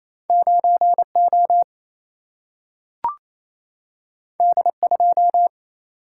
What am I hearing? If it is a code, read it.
Morse code, heard as a keyed tone, about 28 wpm, 700 Hz. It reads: 9 O D 2
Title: Single-Letter Single-Number 28wpm